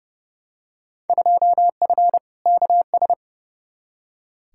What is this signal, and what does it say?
Morse code, heard as a keyed tone, about 30 wpm, 700 Hz. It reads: 2F KS
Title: Field Day Exchanges - 30wpm